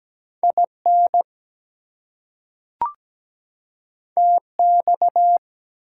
Morse code, heard as a keyed tone, about 17 wpm, 700 Hz. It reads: IN TX
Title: US States and Canadian Provinces 17wpm